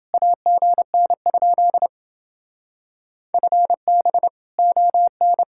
Morse code, heard as a keyed tone, about 29 wpm, 700 Hz. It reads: AGN? FB ON